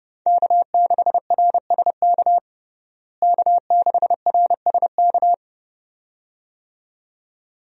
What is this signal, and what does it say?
Morse code, heard as a keyed tone, about 30 wpm, 700 Hz. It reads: K6RSK K6RSK
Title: USA Call Signs - 5 Characters - Repeated 2x - 30wpm